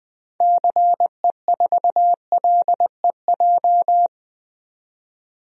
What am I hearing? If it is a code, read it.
Morse code, heard as a keyed tone, about 20 wpm, 700 Hz. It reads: CE4LEJ
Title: DXCC-100 Call Signs - 6 Characters - 20wpm